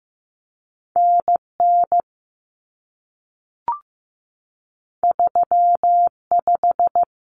Morse code, heard as a keyed tone, about 15 wpm, 700 Hz. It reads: NN 35
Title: Common Cut Numbers - 2-Digit 15wpm